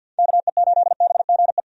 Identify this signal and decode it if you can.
Morse code, heard as a keyed tone, about 52 wpm, 700 Hz. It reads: KE8BGE
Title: USA Call Signs - 6 Characters - 50wpm